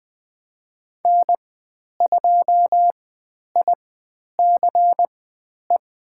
Morse code, heard as a keyed tone, about 20 wpm, spaced as for 10 wpm, 700 Hz. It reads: N2ICE
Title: 095 Lesson 16g - Letter c - Call Signs 20wpm